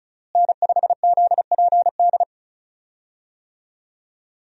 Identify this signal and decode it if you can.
Morse code, heard as a keyed tone, about 34 wpm, 700 Hz. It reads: N5ZPD
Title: USA Call Signs - 5 Characters - Repeated 2x - 35wpm